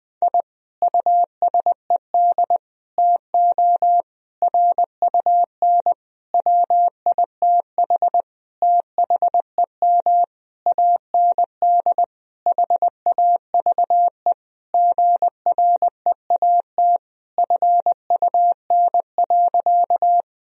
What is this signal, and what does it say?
Morse code, heard as a keyed tone, about 20 wpm, 700 Hz. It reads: I USED TO RUN WITH THEM AND HAVE GREAT FUN.
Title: Sentences from Top 1000 Words - Review Part 4 - 20wpm